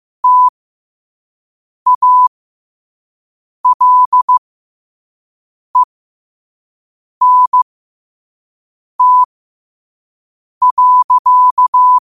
Morse code, heard as a keyed tone, about 15 wpm, spaced as for 5 wpm, 1000 Hz. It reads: TALENT.
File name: qotdfiles / 5wpm